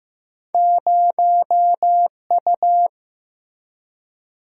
Morse code, heard as a keyed tone, about 15 wpm, 700 Hz. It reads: 0U
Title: Single-Letter Single-Number 15wpm